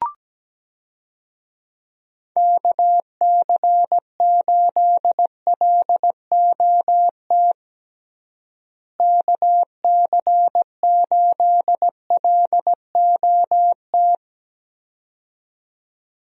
Morse code, heard as a keyed tone, about 17 wpm, 700 Hz. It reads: KC8LOT KC8LOT
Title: USA Call Signs - 6 Characters - Repeated 2x - 17wpm